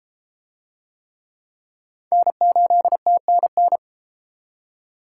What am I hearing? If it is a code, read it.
Morse code, heard as a keyed tone, about 32 wpm, 700 Hz. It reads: N8TNN